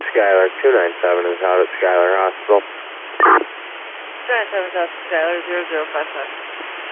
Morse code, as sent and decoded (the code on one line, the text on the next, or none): none